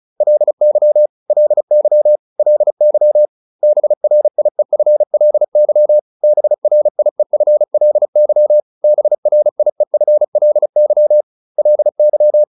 Morse code, heard as a keyed tone, about 34 wpm, 600 Hz. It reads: LY LY LY BRIEFLY BRIEFLY BRIEFLY LY
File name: LY7 35